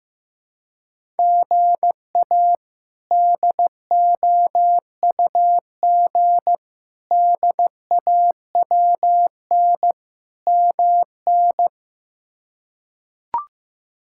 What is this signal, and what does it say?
Morse code, heard as a keyed tone, about 15 wpm, 700 Hz. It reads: GA DOUG DAWN MN
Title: K1USN SST QSO Practice - 15wpm